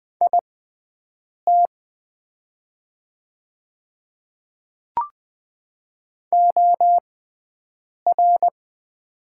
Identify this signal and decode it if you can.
Morse code, heard as a keyed tone, about 20 wpm, spaced as for 7 wpm, 690 Hz. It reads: IT OR
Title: 6x Character Spacing - Top 100 Words - 20wpm